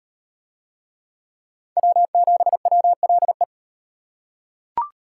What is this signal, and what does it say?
Morse code, heard as a keyed tone, about 33 wpm, 700 Hz. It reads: W7WLE